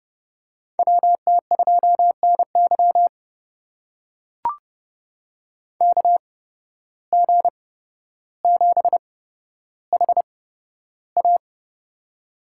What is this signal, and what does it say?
Morse code, heard as a keyed tone, about 30 wpm, spaced as for 13 wpm, 700 Hz. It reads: WT2NY KG7HA
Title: USA Call Signs - 5 Characters - 8x Character Spacing - 30wpm